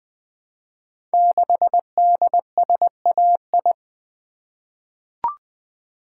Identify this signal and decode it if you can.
Morse code, heard as a keyed tone, about 20 wpm, 700 Hz. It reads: 6DSAI
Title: Code Groups - 5 Letter:Numbers - 20wpm